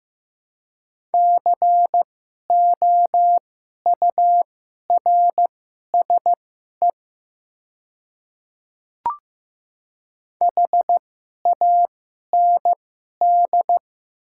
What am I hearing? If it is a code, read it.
Morse code, heard as a keyed tone, about 15 wpm, spaced as for 11 wpm, 700 Hz. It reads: COURSE HAND
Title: 2x Character Spacing - Top 500 Words - 15wpm